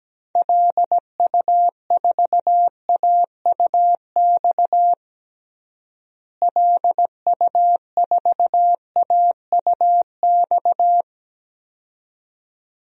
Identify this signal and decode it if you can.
Morse code, heard as a keyed tone, about 17 wpm, 700 Hz. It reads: LU4AUX LU4AUX
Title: DXCC-100 Call Signs - 6 Characters - Repeated 2x - 17wpm